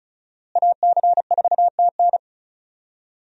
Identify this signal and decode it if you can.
Morse code, heard as a keyed tone, about 36 wpm, 700 Hz. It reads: AC4TN
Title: USA Call Signs - 5 Characters - Repeated 3x - 35wpm